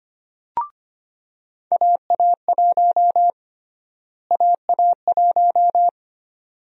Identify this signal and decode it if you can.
Morse code, heard as a keyed tone, about 25 wpm, 700 Hz. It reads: AA1 AA1
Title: USA Call Sign Prefixes - Repeated 2x - 25wpm